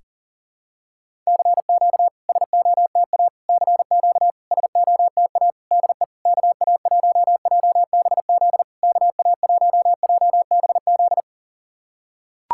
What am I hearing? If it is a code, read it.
Morse code, heard as a keyed tone, about 40 wpm, 700 Hz. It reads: CQ SOTA CQ SOTA DE KA1JBZ KA1JBZ
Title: SOTA QSO Practice 40wpm